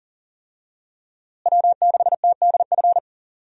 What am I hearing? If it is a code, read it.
Morse code, heard as a keyed tone, about 40 wpm, 700 Hz. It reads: W6TDF
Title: USA Call Signs - 5 Characters - Repeated 3x - 40wpm